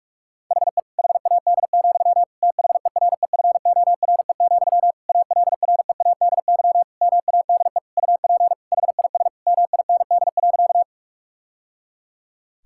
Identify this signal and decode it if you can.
Morse code, heard as a keyed tone, about 46 wpm, 700 Hz. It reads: HE HAD, THEREFORE, ALREADY MADE UP HIS MIND.
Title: Sentences from Top 1000 Words - Review Part 5 - 45wpm